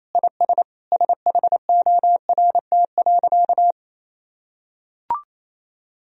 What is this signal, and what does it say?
Morse code, heard as a keyed tone, about 28 wpm, 700 Hz. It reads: IS SHORT.
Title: Sentences from Top 500 Words and Abbreviations 28wpm